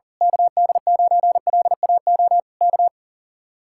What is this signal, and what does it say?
Morse code, heard as a keyed tone, about 38 wpm, 700 Hz. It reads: KD9LAO K
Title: Speed-Racing Common QSO Phrases - 40wpm